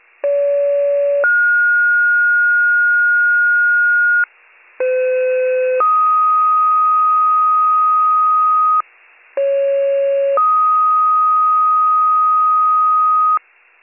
Two-tone page, heard as a one-second tone, then a three-second tone, about 570 then 1430 Hz, about 510 then 1250 Hz, about 550 then 1250 Hz. It is heard from 0.2 to 13.4 s.